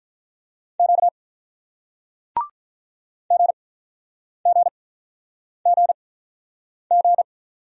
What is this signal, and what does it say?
Morse code, heard as a keyed tone, about 48 wpm, spaced as for 9 wpm, 700 Hz. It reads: Q GGGG